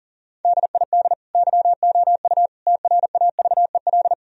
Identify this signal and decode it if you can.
Morse code, heard as a keyed tone, about 37 wpm, 700 Hz. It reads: DID YOU TRAVEL
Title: Sentences from Top 1000 Words - Review Part 2 - 40wpm